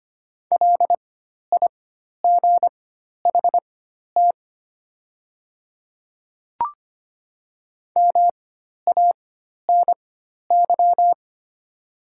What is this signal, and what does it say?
Morse code, heard as a keyed tone, about 25 wpm, spaced as for 12 wpm, 700 Hz. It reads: LIGHT MANY
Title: 4x Character Spacing - Top 200 Words - 25wpm